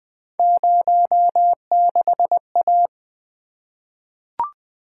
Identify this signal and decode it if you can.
Morse code, heard as a keyed tone, about 20 wpm, 700 Hz. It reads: 06A
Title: Three Letter-Number Combinations - 20wpm